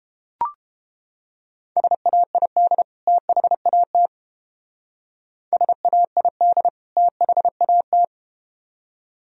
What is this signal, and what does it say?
Morse code, heard as a keyed tone, about 31 wpm, 700 Hz. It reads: SAID THAT SAID THAT